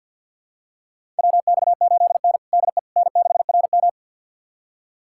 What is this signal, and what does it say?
Morse code, heard as a keyed tone, about 52 wpm, 700 Hz. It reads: WX8N DE N6RM
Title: USA Callsign DE Callsign - 4 Characters - 50wpm